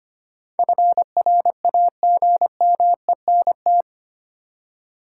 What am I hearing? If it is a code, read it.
Morse code, heard as a keyed tone, about 25 wpm, 700 Hz. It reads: FRAGMENT
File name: Eight Letter Words - 25wpm